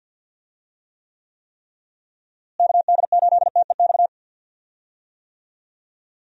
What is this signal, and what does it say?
Morse code, heard as a keyed tone, about 45 wpm, 700 Hz. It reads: KD8TEX
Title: USA Call Signs - 6 Characters - Repeated 3x - 50wpm